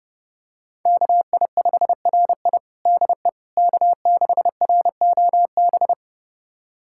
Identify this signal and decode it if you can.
Morse code, heard as a keyed tone, about 32 wpm, 700 Hz. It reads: KI5RI DE K6ROB
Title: USA Callsign DE Callsign - 5 Characters - 30wpm